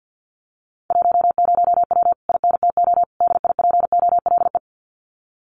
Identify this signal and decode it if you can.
Morse code, heard as a keyed tone, about 48 wpm, 700 Hz. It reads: 10W INTO DIPOLE
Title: Common QSO Phrases Encore 50wpm